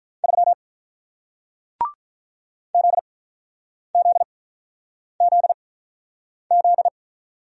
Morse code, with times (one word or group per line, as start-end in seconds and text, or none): 0.2-0.5 3
2.7-6.9 ZZZZ